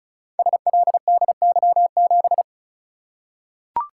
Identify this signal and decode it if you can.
Morse code, heard as a keyed tone, about 36 wpm, 700 Hz. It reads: SLDY7
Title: Code Groups - 5 Letter:Numbers - 35wpm